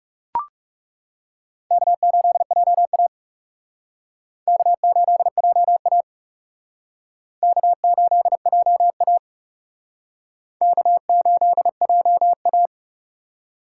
Morse code, heard as a keyed tone, about 37 wpm, 700 Hz. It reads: K8JA K8JA K8JA K8JA
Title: Speed Racing - USA 1x2 Call Signs 30wpm